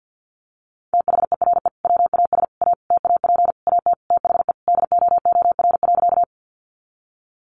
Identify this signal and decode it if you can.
Morse code, heard as a keyed tone, about 50 wpm, 700 Hz. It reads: THERE WAS A TAP AT THE DOOR.
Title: Sentences from Top 1400 Words 50wpm